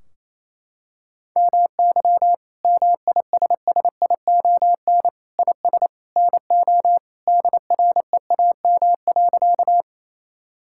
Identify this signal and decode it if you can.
Morse code, heard as a keyed tone, about 28 wpm, 700 Hz. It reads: MY MISSION IS NO DREAM.
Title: Sentences from Top 1600 Words 28wpm